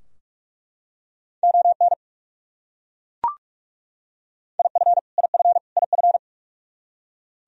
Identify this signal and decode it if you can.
Morse code, heard as a keyed tone, about 42 wpm, 700 Hz. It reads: ON IF IF IF
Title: Top 100 Words Repeated 45wpm